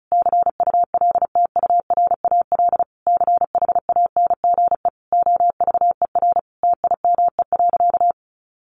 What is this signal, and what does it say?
Morse code, heard as a keyed tone, about 34 wpm, 700 Hz. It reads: CULTURAL CHANGE OVER TIME.
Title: Sentences from Top 1500 Words 35wpm